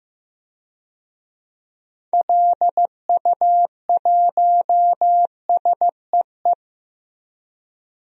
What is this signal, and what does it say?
Morse code, heard as a keyed tone, about 15 wpm, 700 Hz. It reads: LU1SEE
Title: DXCC-100 Call Signs - 6 Characters - 15wpm